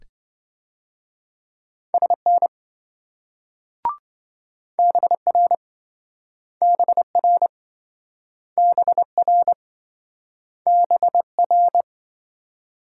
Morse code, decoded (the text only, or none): SN BR BR BR BR